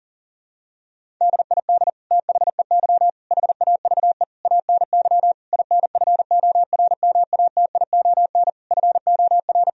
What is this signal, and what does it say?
Morse code, heard as a keyed tone, about 40 wpm, 700 Hz. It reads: DID THEY HAVE ANY INFORMATION FOR
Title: Sentences from Top 1000 Words - Review Part 3 - 40wpm